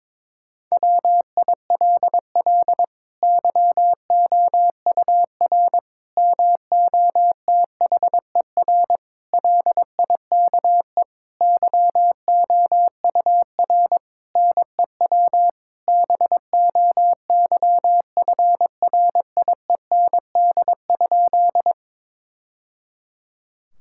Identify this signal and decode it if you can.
Morse code, heard as a keyed tone, about 22 wpm, 700 Hz. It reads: WILL YOUR MOTHER LIKE YOUR NEW BOYFRIEND?
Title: Sentences from Top 1000 Words - Review Part 5 - 22wpm